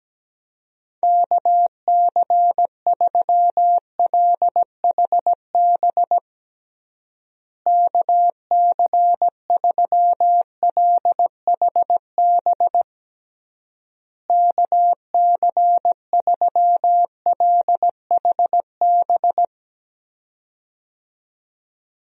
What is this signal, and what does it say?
Morse code, heard as a keyed tone, about 17 wpm, 700 Hz. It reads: KC3LHB KC3LHB KC3LHB
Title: USA Call Signs - 6 Characters - Repeated 3x - 17wpm